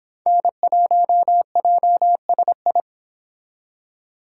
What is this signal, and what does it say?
Morse code, heard as a keyed tone, about 26 wpm, 700 Hz. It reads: N1JSI